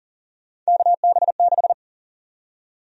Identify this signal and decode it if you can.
Morse code, heard as a keyed tone, about 40 wpm, 700 Hz. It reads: KB6
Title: USA Call Sign Prefixes - Repeated 2x - 40wpm